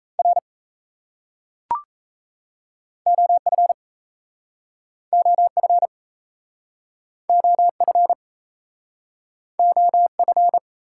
Morse code, heard as a keyed tone, about 40 wpm, 700 Hz. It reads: R OF OF OF OF